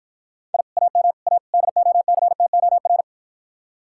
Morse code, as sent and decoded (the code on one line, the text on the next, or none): .. / .- -- / .- / -.. --- -.-. - --- .-.
I AM A DOCTOR